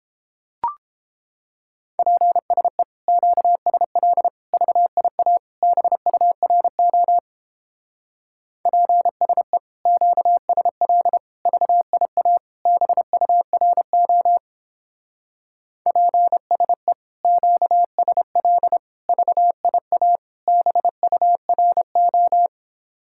Morse code, heard as a keyed tone, about 31 wpm, 700 Hz. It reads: PSE QSL VIA BURO PSE QSL VIA BURO PSE QSL VIA BURO